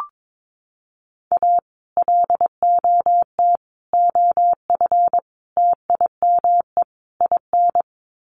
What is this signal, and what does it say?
Morse code, heard as a keyed tone, about 22 wpm, 700 Hz. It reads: A LOT OF TIME IN
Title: Sets of 5 Words Encore2 22wpm